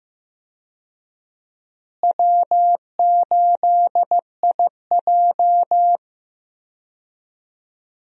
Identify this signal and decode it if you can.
Morse code, heard as a keyed tone, about 16 wpm, 700 Hz. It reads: W8IJ
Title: USA Call Signs - 4 Characters - Repeated 3x - 15wpm